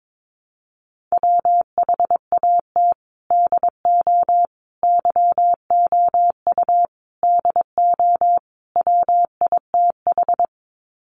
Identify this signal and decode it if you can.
Morse code, heard as a keyed tone, about 22 wpm, 700 Hz. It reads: WHAT DO YOU DO WITH
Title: Sets of 5 Words Encore2 22wpm